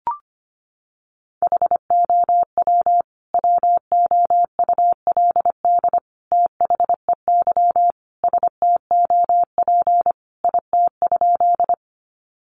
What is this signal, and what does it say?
Morse code, heard as a keyed tone, about 25 wpm, 700 Hz. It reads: HOW WOULD THEY STOP IT?
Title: Sentences from Top 300 Words 25wpm